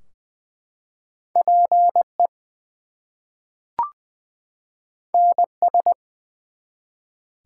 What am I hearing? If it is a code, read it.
Morse code, heard as a keyed tone, about 20 wpm, 700 Hz. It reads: PE NS
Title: US States and Canadian Provinces 20wpm